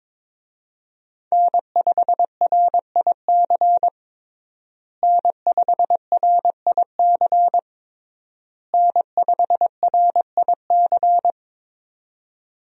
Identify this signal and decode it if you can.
Morse code, heard as a keyed tone, about 22 wpm, 700 Hz. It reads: N5RIC N5RIC N5RIC
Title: USA Call Signs - 5 Characters - Repeated 3x - 22wpm